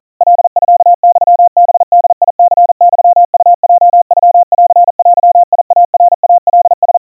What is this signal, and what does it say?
Morse code (Japanese Wabun code), heard as a keyed tone, about 40 wpm, 700 Hz. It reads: カミユハボニユウヲノンデイナイカラ
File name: kawa / Morse sample / kamyu 150 cpm 700 Hz Crispy